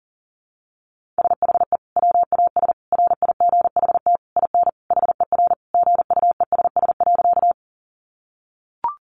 Morse code, heard as a keyed tone, about 40 wpm, 700 Hz. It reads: SHE WAS RIGHT IN HER GUESS.
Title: Sentences from Top 1500 Words 40wpm